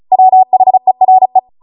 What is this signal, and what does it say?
Morse code, heard as a keyed tone, about 36 wpm, 750 Hz. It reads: WHERE